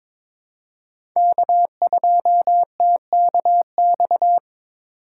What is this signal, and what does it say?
Morse code, heard as a keyed tone, about 21 wpm, 700 Hz. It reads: K2TKX
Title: USA Call Signs - 2x Character Spacing - 22wpm